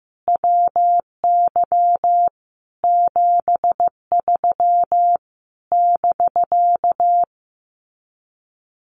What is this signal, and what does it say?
Morse code, heard as a keyed tone, about 15 wpm, 700 Hz. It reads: WY 73 <BK>
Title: Common QSO Phrases Encore 15wpm